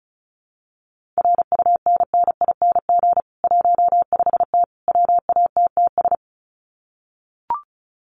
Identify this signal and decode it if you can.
Morse code, heard as a keyed tone, about 34 wpm, 700 Hz. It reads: RUNNING 15T WATTS
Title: Common QSO Phrases 35wpm